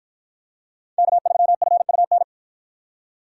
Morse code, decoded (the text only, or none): K3FUN